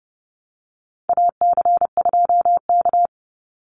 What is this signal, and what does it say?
Morse code, heard as a keyed tone, about 30 wpm, 700 Hz. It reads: AC2K
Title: Call Signs - 30wpm